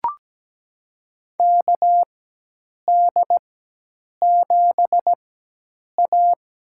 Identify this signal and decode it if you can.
Morse code, heard as a keyed tone, about 17 wpm, spaced as for 8 wpm, 700 Hz. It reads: KD7A
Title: USA Call Signs - 4x Character Spacing - 17wpm